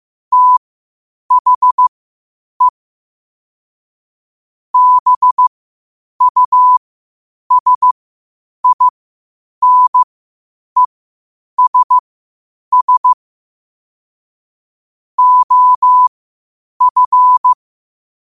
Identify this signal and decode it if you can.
Morse code, heard as a keyed tone, about 15 wpm, spaced as for 8 wpm, 1000 Hz. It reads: THE BUSINESS OF
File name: 8wpm